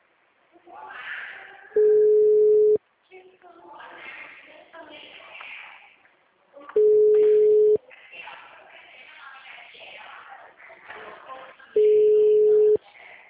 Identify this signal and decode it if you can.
Ringback tone (Europe, 425 Hz).